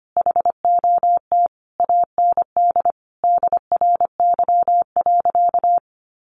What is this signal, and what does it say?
Morse code, heard as a keyed tone, about 25 wpm, 700 Hz. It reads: HOT AND DRY.
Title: Sentences from Top 1500 Words 25wpm